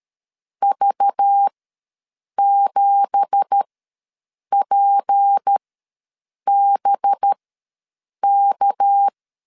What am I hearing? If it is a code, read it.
Morse code, heard as a keyed tone, about 12 wpm, spaced as for 7 wpm, 800 Hz. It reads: V7PBK
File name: LEZIONI / 9 2867 ws 12wpm 3431